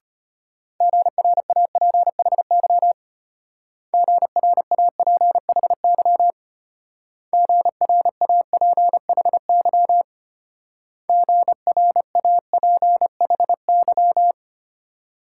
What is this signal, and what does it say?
Morse code, heard as a keyed tone, about 30 wpm, 700 Hz. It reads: GRAPHY GRAPHY GRAPHY GRAPHY
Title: Speed-Racing Multi-letter Phonemes - 25wpm